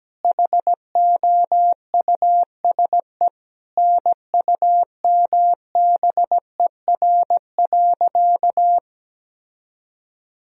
Morse code, heard as a keyed tone, about 17 wpm, 700 Hz. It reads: HOUSE NUMBER.
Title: Sentences from Top 1000 Words - Review Part 1 - 17wpm